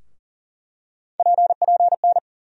Morse code, heard as a keyed tone, about 40 wpm, 700 Hz. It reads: PPN
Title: USA Call Sign Postfixes - 3 Letters 40wpm